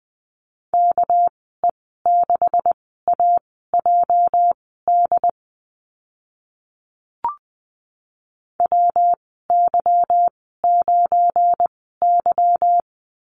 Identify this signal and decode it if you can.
Morse code, heard as a keyed tone, about 20 wpm, spaced as for 14 wpm, 700 Hz. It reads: KE6AJD WY9Y
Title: 219 Lesson 35g - Number 0 - Call Signs 20wpm